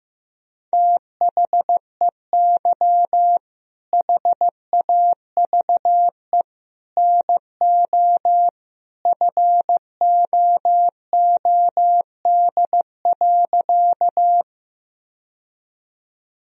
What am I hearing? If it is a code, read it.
Morse code, heard as a keyed tone, about 15 wpm, 700 Hz. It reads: THEY HAVE NO FOOD.
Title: Sentences from Top 1000 Words - Review Part 1 - 15wpm